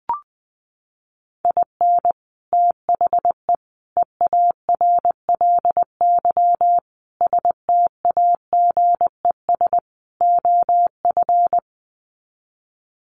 Morse code, heard as a keyed tone, about 20 wpm, 700 Hz. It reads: IN THE EARLY STAGES OF
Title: Sets of 5 Words Encore 20wpm